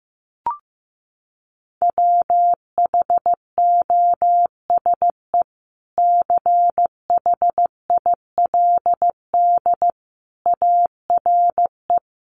Morse code, heard as a keyed tone, about 15 wpm, 700 Hz. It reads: WHOSE CHILD ARE
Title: Sentences from Top 1200 Words 15wpm